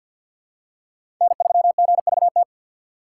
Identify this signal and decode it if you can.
Morse code, heard as a keyed tone, about 53 wpm, 700 Hz. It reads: N3GVT
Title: USA Call Signs - 5 Characters - 3x Character Spacing - 50wpm